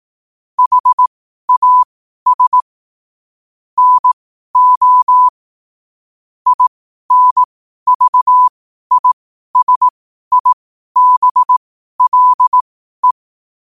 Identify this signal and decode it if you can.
Morse code, heard as a keyed tone, about 18 wpm, spaced as for 12 wpm, 1000 Hz. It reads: HAS NO INVISIBLE
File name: qotdfiles / 12wpm